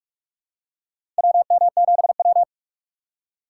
Morse code, heard as a keyed tone, about 42 wpm, 700 Hz. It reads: WM7W
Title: USA Call Signs - 3x Character Spacing - 45wpm